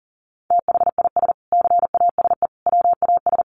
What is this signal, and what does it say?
Morse code, heard as a keyed tone, about 40 wpm, 700 Hz. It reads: THIS CASE WAS
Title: Sentences from Top 1100 Words 40wpm